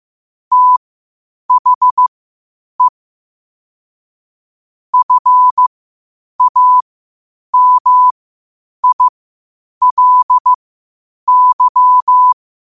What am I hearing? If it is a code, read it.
Morse code, heard as a keyed tone, about 15 wpm, spaced as for 8 wpm, 1000 Hz. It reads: THE FAMILY
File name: qotdfiles / 8wpm